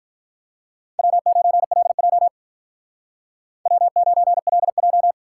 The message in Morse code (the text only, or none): W9LJ W9LJ